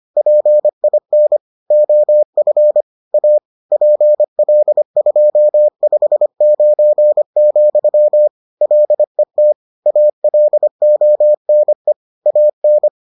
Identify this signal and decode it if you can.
Morse code, heard as a keyed tone, about 25 wpm, 600 Hz. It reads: PIN OF A PL259, LET ALONE AN